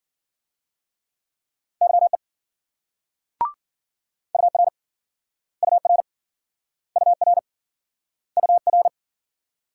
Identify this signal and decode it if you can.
Morse code, heard as a keyed tone, about 54 wpm, 700 Hz. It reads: YE UR UR UR UR